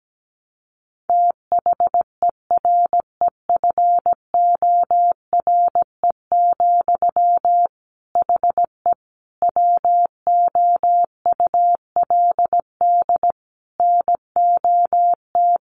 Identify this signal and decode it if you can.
Morse code, heard as a keyed tone, about 17 wpm, 700 Hz. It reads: THEREFORE, HE WOULD NOT
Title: Sentences from Top 800 Words 17wpm